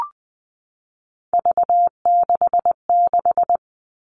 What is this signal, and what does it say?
Morse code, heard as a keyed tone, about 20 wpm, 700 Hz. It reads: V66
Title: All Cut Numbers - 3-Digit 20wpm